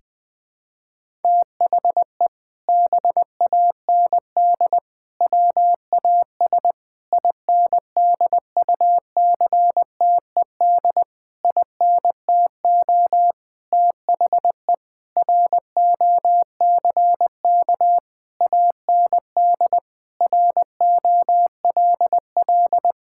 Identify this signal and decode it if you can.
Morse code, heard as a keyed tone, about 20 wpm, 700 Hz. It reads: THE BAND WAS INDUCTED INTO THE ROCK AND ROLL
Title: Sentences with 6-Word Phrases - Spoken Beginning - 20wpm